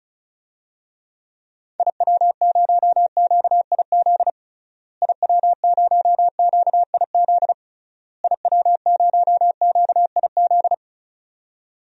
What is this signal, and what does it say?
Morse code, heard as a keyed tone, about 36 wpm, 700 Hz. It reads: IW0QIZ IW0QIZ IW0QIZ
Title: DXCC-100 Call Signs - 6 Characters - Repeated 3x - 35wpm